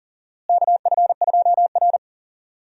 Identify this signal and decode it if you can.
Morse code, heard as a keyed tone, about 40 wpm, 700 Hz. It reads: KF2R
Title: USA Call Signs - 4 Characters - Repeated 2x - 40wpm